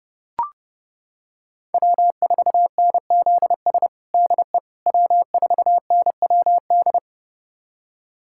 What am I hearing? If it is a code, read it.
Morse code, heard as a keyed tone, about 30 wpm, 700 Hz. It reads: W4NZS DE W4NWD
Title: USA Callsign DE Callsign - 5 Characters - 30wpm